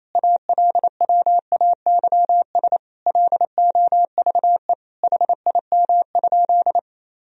Morse code, heard as a keyed tone, about 28 wpm, 700 Hz. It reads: ALWAYS LOVE HIM?
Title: Sentences from Top 400 Words and Abbreviations 28wpm